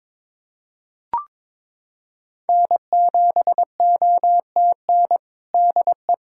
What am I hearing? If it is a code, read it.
Morse code, heard as a keyed tone, about 22 wpm, 700 Hz. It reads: N7OTN DE
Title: USA Callsign DE Callsign - 5 Characters - 22wpm